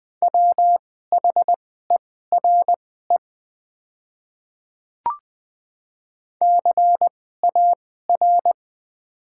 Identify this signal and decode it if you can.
Morse code, heard as a keyed tone, about 21 wpm, spaced as for 15 wpm, 690 Hz. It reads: WHERE CAR
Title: 2x Character Spacing - Top 500 Words - 20wpm